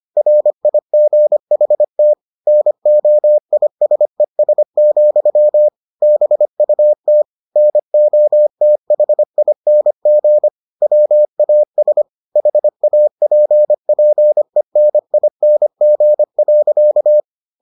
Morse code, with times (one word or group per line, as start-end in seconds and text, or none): 0.2-2.1 RIGHT
2.5-5.7 NOISES,
6.0-7.2 BUT
7.6-10.5 NOTHING
10.8-12.0 WAS
12.4-17.2 HAPPENING.